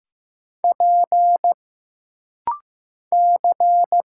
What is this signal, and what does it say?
Morse code, heard as a keyed tone, about 15 wpm, 700 Hz. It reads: P C
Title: Single Letter-Number Alt - Rapid-Fire 15wpm